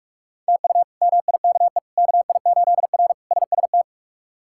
Morse code, heard as a keyed tone, about 48 wpm, 700 Hz. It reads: TU MIKE KI8R SST
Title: K1USN SST QSO Practice - 45wpm